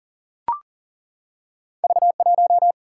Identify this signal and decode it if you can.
Morse code, heard as a keyed tone, about 42 wpm, 700 Hz. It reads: V1
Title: Single-Letter Single-Number 40wpm